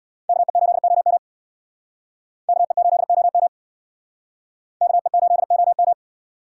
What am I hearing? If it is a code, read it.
Morse code, heard as a keyed tone, about 62 wpm, 700 Hz. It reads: KE8OM KE8OM KE8OM
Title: Speed Racing - USA 2x2 Call Signs 50wpm